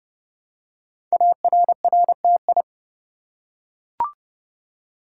Morse code, heard as a keyed tone, about 29 wpm, 700 Hz. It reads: ARRTI
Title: Code Groups - 5 Letters - 30wpm